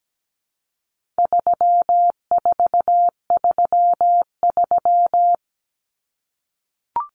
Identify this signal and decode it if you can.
Morse code, heard as a keyed tone, about 17 wpm, 700 Hz. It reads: 3433
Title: Numbers - 4-Digit 17wpm